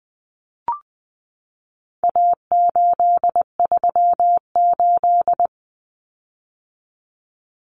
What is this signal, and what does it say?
Morse code, heard as a keyed tone, about 20 wpm, 700 Hz. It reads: A838
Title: Common Cut Numbers - 4-Digit 20wpm